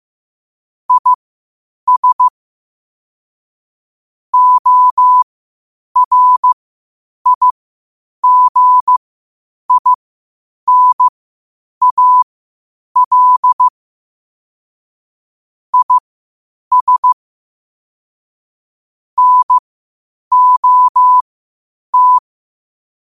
Morse code, heard as a keyed tone, about 15 wpm, spaced as for 8 wpm, 1000 Hz. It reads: IS ORIGINAL IS NOT
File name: qotdfiles / 8wpm